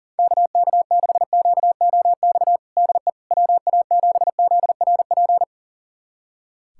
Morse code, heard as a keyed tone, about 40 wpm, 700 Hz. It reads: KK6QOX DE WA7ZRP
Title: USA Callsign DE Callsign - 6 Characters - 40wpm